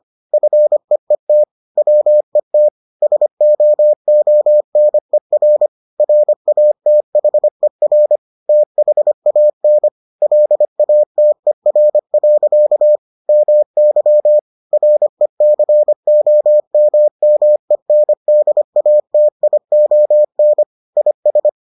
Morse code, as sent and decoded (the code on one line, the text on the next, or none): ..-. . - / .-- . - / ... --- --- -. .-. / .-. .- - .... .-. / - .... .- -. / .-.. .- - . .-. .-.-.- / -- -.-- / .-. -.-. --- -- -- . -. -.. .- - .. --- -. / .. ...
FEET WET SOONER RATHER THAN LATER. MY RECOMMENDATION IS